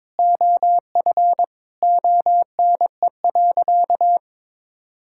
Morse code, heard as a keyed tone, about 21 wpm, 700 Hz. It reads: OF ONE.
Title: Sentences from Top 1000 Words - Review Part 5 - 22wpm